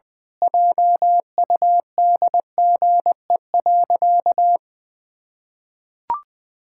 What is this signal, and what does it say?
Morse code, heard as a keyed tone, about 20 wpm, 700 Hz. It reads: JUDGE.
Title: Pangram Challenge v2 - 20wpm